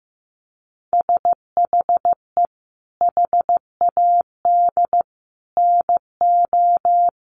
Morse code, heard as a keyed tone, about 15 wpm, 700 Hz. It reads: SHE HAD NO
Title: Sentences from Top 1500 Words 15wpm